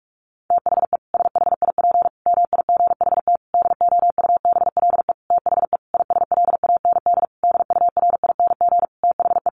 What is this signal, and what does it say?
Morse code, heard as a keyed tone, about 44 wpm, 700 Hz. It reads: THE SHIP MIGHT DOUBLE THE ISLAND DURING THE
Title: Sentences from Top 1400 Words 45wpm